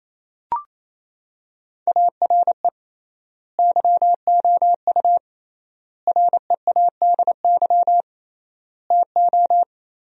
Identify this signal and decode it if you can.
Morse code, heard as a keyed tone, about 28 wpm, 700 Hz. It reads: ARE YOU READY TO
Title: 3x Word-Spacing - Sentences from Top 400 Words - 28wpm